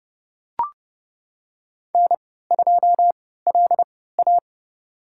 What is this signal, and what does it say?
Morse code, heard as a keyed tone, about 30 wpm, spaced as for 17 wpm, 700 Hz. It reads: N2LA
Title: USA Call Signs - 4 Characters - 3x Character Spacing - 30wpm